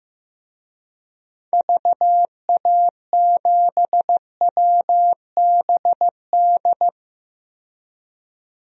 Morse code, heard as a keyed tone, about 15 wpm, 700 Hz. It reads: VA7WBD
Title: DXCC-100 Call Signs - 6 Characters - 15wpm